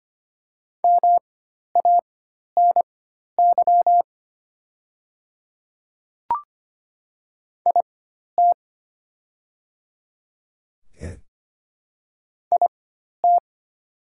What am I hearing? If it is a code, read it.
Morse code, heard as a keyed tone, about 25 wpm, spaced as for 12 wpm, 700 Hz. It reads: MANY IT IT
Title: 4x Character Spacing - Top 200 Words - 25wpm